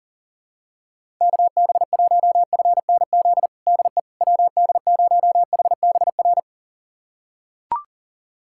Morse code, heard as a keyed tone, about 40 wpm, 700 Hz. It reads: KB1FNZ DE WD0HBR
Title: USA Callsign DE Callsign - 6 Characters - 40wpm